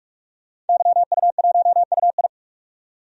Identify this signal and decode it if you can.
Morse code, heard as a keyed tone, about 43 wpm, 700 Hz. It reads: YU1UI
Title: DXCC-100 Call Signs - 5 Characters - 45wpm